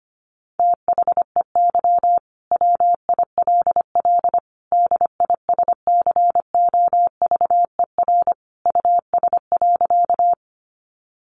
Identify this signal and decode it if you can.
Morse code, heard as a keyed tone, about 25 wpm, 690 Hz. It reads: THEY WILL DISCOVER US.
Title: Sentences from Top 1100 Words 25wpm